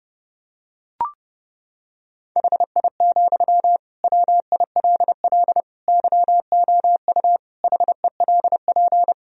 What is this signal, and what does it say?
Morse code, heard as a keyed tone, about 30 wpm, 700 Hz. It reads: HI, WILL YOU HELP
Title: Sentences from Top 1000 Words - Review Part 3 - 30wpm